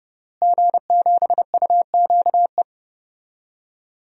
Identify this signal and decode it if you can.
Morse code, heard as a keyed tone, about 30 wpm, 700 Hz. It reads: G7UQE
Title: DXCC-100 Call Signs - 5 Characters - Repeated 2x - 30wpm